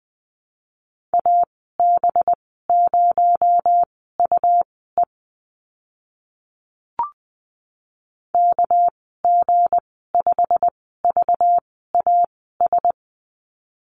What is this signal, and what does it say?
Morse code, heard as a keyed tone, about 20 wpm, spaced as for 15 wpm, 700 Hz. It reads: AB0UE KG5VAS
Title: 238 Lesson 38g - Letter q - Call Signs 20wpm